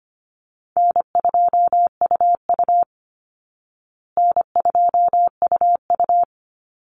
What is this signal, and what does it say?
Morse code, heard as a keyed tone, about 25 wpm, 700 Hz. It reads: N2UU N2UU
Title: Easy Call Signs Repeated 3x - 25wpm